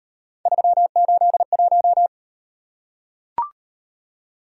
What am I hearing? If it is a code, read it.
Morse code, heard as a keyed tone, about 37 wpm, 700 Hz. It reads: 381